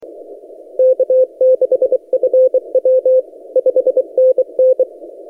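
Morse code, heard as a keyed tone, about 23 wpm, 520 Hz. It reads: K6FW 5NN
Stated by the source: XR0YG DX signal on 17 metres CW, posted by N6ML.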